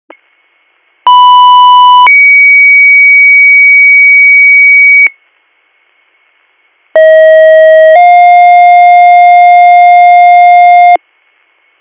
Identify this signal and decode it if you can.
Two-tone page, heard as a one-second tone, then a three-second tone, about 990 then 2250 Hz, about 640 then 710 Hz. It is heard from 1.1 to 11.0 s.